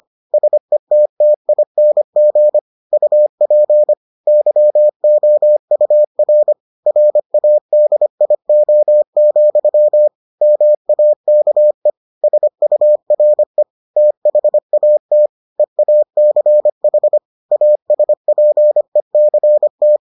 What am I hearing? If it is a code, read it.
Morse code, heard as a keyed tone, about 25 wpm, 600 Hz. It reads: SETTING UP YOUR RADIO, MAKE SURE THAT EACH ASPECT